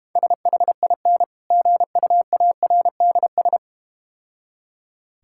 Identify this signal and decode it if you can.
Morse code, heard as a keyed tone, about 32 wpm, 700 Hz. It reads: SHIN GUARDS